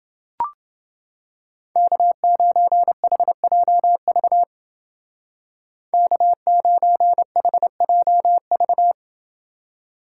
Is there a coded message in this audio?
Yes: Morse code from 1.8 to 8.9 s, heard as a keyed tone, about 29 wpm, 700 Hz. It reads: K9HJV K9HJV